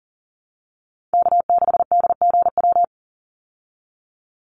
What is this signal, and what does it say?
Morse code, heard as a keyed tone, about 40 wpm, 700 Hz. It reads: K6DGW
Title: Call Signs Repeated 3x - 40wpm